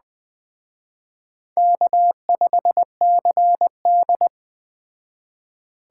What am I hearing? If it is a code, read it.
Morse code, heard as a keyed tone, about 20 wpm, 700 Hz. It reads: K5CD